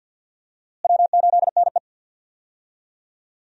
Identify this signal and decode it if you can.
Morse code, heard as a keyed tone, about 46 wpm, 700 Hz. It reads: W8NE
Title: USA Call Signs - 4 Characters - Repeated 3x - 50wpm